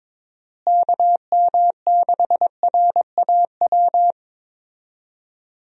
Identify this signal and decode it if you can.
Morse code, heard as a keyed tone, about 22 wpm, 700 Hz. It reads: KM6RAW
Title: USA Call Signs - 6 Characters - Repeated 3x - 22wpm